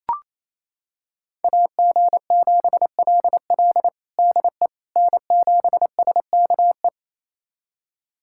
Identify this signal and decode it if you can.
Morse code, heard as a keyed tone, about 28 wpm, 700 Hz. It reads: AG7LL DE N7SKE
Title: USA Callsign DE Callsign - 5 Characters - 28wpm